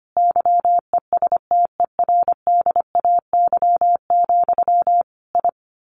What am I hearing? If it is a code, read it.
Morse code, heard as a keyed tone, about 25 wpm, 700 Hz. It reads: YESTERDAY, I
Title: A Sound Of Thunder 25wpm